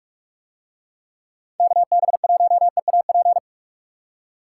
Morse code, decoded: KB1EAP

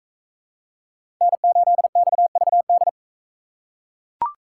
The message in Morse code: N8XVD

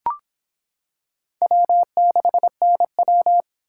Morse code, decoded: W6NW

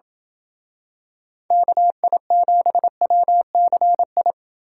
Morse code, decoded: KI7WCI

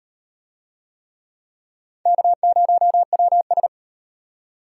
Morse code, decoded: K0WS

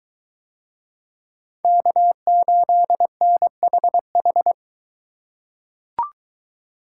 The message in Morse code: K8NHH

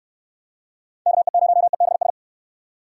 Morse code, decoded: KE0EBS

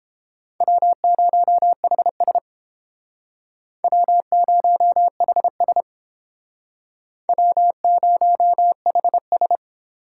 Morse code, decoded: W0HS W0HS W0HS

